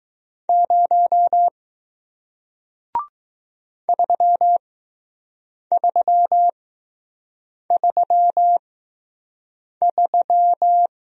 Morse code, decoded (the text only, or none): 0 3333